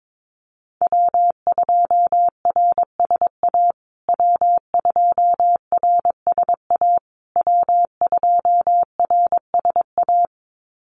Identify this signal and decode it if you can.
Morse code, heard as a keyed tone, about 22 wpm, 700 Hz. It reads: W2RSA W2RSA W2RSA
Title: USA Call Signs Repeated 3x 22wpm